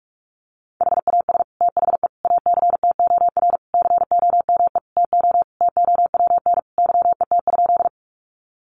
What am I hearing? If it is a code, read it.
Morse code, heard as a keyed tone, about 44 wpm, 700 Hz. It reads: HAS THE ACTOR COME TO TOWN YET?
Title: Sentences from Top 1500 Words 45wpm